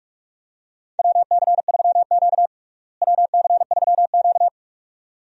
Morse code, heard as a keyed tone, about 42 wpm, 700 Hz. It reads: WC3Q WC3Q
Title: USA Call Signs - 4 Characters - Repeated 2x - 45wpm